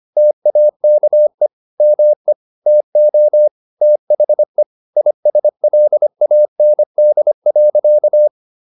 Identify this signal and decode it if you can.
Morse code, heard as a keyed tone, about 25 wpm, 600 Hz. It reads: TAKE ME TO THE ISLAND.